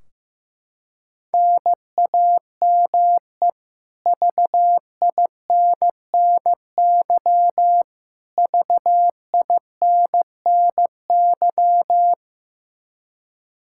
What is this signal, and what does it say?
Morse code, heard as a keyed tone, about 15 wpm, 700 Hz. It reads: NAME VINNY VINNY
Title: Practice QSO - Exchange 1 - Beginning - 15wpm